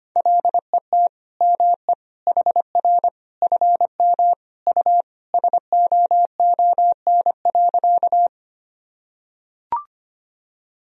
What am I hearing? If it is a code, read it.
Morse code, heard as a keyed tone, about 25 wpm, 700 Hz. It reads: LET ME HR FM U SOON.
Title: Sentences from Top 400 Words and Abbreviations 25wpm